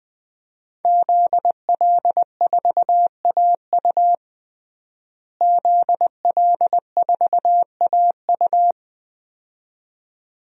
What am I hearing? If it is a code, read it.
Morse code, heard as a keyed tone, about 20 wpm, 700 Hz. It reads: ZL4AU ZL4AU
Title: DXCC-100 Call Signs - 5 Characters - Repeated 2x - 20wpm